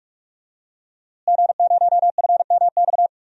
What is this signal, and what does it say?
Morse code, heard as a keyed tone, about 45 wpm, 700 Hz. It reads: G0FMX
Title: DXCC-100 Call Signs - 5 Characters - Repeated 2x - 45wpm